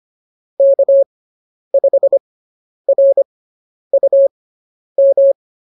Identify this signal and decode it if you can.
Morse code, heard as a keyed tone, about 25 wpm, spaced as for 10 wpm, 550 Hz. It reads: K5RUM